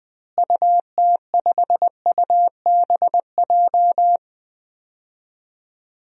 Morse code, heard as a keyed tone, about 21 wpm, 700 Hz. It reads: UT5UBJ
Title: DXCC-100 Call Signs - 6 Characters - Repeated 3x - 20wpm